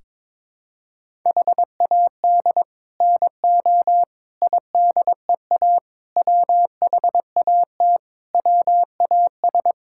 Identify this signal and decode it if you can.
Morse code, heard as a keyed tone, about 21 wpm, 700 Hz. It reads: HAD NO IDEA WHAT WAS
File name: practice / 22wpm / Sets of 7 Words - 22wpm